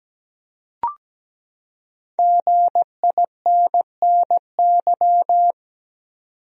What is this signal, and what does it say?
Morse code, heard as a keyed tone, about 17 wpm, 700 Hz. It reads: GINNY
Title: USA Common Names - Max 5 Letters - 17wpm